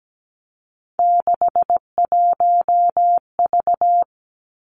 Morse code, heard as a keyed tone, about 17 wpm, 700 Hz. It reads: 61V
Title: Three Challenging Letter-Number Combinations 17wpm